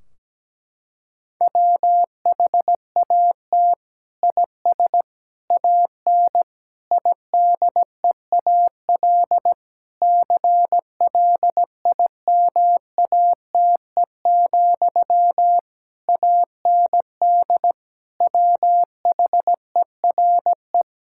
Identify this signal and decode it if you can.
Morse code, heard as a keyed tone, about 17 wpm, 700 Hz. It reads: WHAT IS AN IDEAL CLIMATE, AND WHERE
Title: Sentences from Top 1600 Words 17wpm